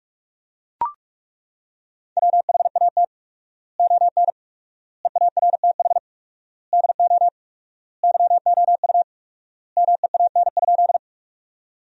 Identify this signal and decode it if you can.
Morse code, heard as a keyed tone, about 41 wpm, 700 Hz. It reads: WHAT ON EARTH DO YOU MEAN?
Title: 4x Word-Spacing - Sentences from Top 200 Words - 45wpm